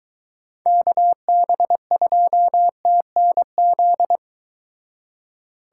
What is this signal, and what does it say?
Morse code, heard as a keyed tone, about 22 wpm, 700 Hz. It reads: KB2TNZ